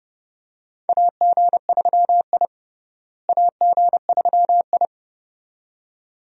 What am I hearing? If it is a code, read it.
Morse code, heard as a keyed tone, about 30 wpm, 700 Hz. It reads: AG3I AG3I
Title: USA Call Signs - 4 Characters - Repeated 2x - 30wpm